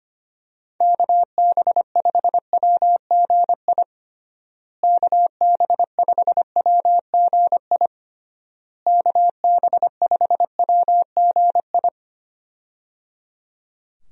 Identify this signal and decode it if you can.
Morse code, heard as a keyed tone, about 25 wpm, 700 Hz. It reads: KB5WGI KB5WGI KB5WGI
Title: USA Call Signs - 6 Characters - Repeated 3x - 25wpm